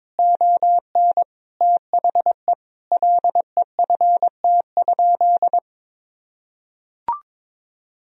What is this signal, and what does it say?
Morse code, heard as a keyed tone, about 22 wpm, 700 Hz. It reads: ON THE LEFT?
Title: Sentences from Top 1000 Words - Review Part 2 - 22wpm